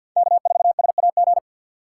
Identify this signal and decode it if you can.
Morse code, heard as a keyed tone, about 52 wpm, 700 Hz. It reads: K4SAG
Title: USA Call Signs - 5 Characters - 4x Character Spacing - 50wpm